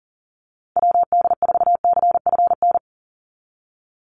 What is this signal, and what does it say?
Morse code, heard as a keyed tone, about 38 wpm, 700 Hz. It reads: WD4CFN
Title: NA SOTA Chaser Callsigns 40wpm